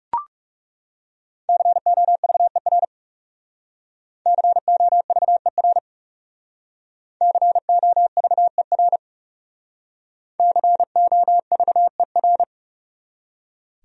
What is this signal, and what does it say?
Morse code, heard as a keyed tone, about 39 wpm, 700 Hz. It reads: COVER COVER COVER COVER